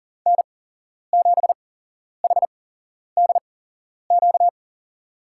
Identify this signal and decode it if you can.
Morse code, heard as a keyed tone, about 40 wpm, spaced as for 11 wpm, 700 Hz. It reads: N7HDQ
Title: USA Call Signs - 8x Character Spacing - 40wpm